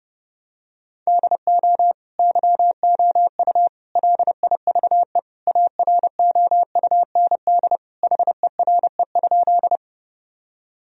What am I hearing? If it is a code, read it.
Morse code, heard as a keyed tone, about 30 wpm, 700 Hz. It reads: DO YOU LIVE AROUND HERE?
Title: Sentences from Top 1000 Words - Review Part 3 - 30wpm